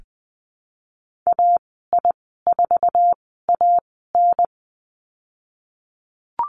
Morse code, heard as a keyed tone, about 20 wpm, spaced as for 15 wpm, 700 Hz. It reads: AI4AN
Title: 055 Lesson 10g - Letter r - Call Signs 20wpm